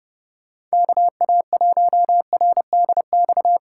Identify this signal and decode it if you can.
Morse code, heard as a keyed tone, about 36 wpm, 700 Hz. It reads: KA1RDX
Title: Speed Racing - USA 2x3 Call Signs 30wpm